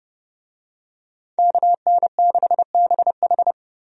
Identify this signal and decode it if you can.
Morse code, heard as a keyed tone, about 30 wpm, 700 Hz. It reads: KN6BH
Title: USA Call Signs - 5 Characters - Repeated 2x - 30wpm